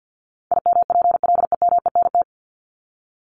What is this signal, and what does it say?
Morse code, heard as a keyed tone, about 50 wpm, 700 Hz. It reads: IMPLEMENT
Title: Top 1501-1750 Words 50wpm